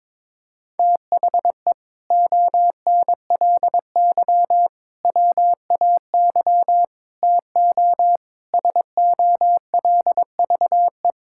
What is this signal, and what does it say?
Morse code, heard as a keyed tone, about 22 wpm, 700 Hz. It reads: THE ONLY WAY TO SOLVE
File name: practice / 22wpm / Sentences with 4-Word Phrases - 22wpm